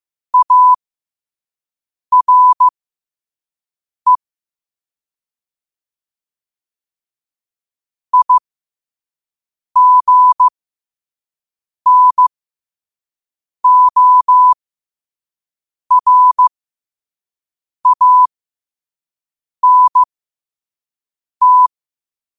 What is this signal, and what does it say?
Morse code, heard as a keyed tone, about 15 wpm, spaced as for 5 wpm, 1000 Hz. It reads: ARE IGNORANT
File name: qotdfiles / 5wpm